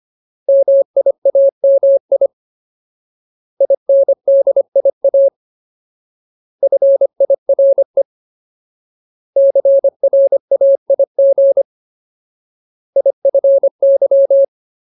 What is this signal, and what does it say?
Morse code, heard as a keyed tone, about 25 wpm, 550 Hz. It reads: MIAMI INDIA FIRE CRAIG IFY
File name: Ultimate Word List / 25WPM Ultimate